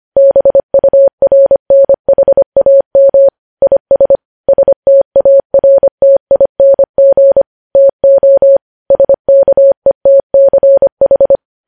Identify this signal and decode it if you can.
Morse code, heard as a keyed tone, about 25 wpm, 550 Hz. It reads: BURNHAM IS STARTING TO SKETCH
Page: news-headlines-in-morse-code-at-25-wpm